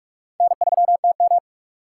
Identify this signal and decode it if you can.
Morse code, heard as a keyed tone, about 46 wpm, 700 Hz. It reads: N3TM
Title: USA Call Signs - 4 Characters - 2x Character Spacing - 45wpm